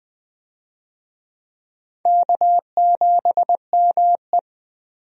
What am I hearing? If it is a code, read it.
Morse code, heard as a keyed tone, about 20 wpm, 700 Hz. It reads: K7ME
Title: USA Call Signs - 4 Characters - 20wpm